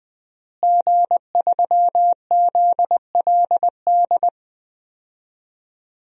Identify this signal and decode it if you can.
Morse code, heard as a keyed tone, about 20 wpm, 700 Hz. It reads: G3ZLD
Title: DXCC-100 Call Signs - 5 Characters - Repeated 3x - 20wpm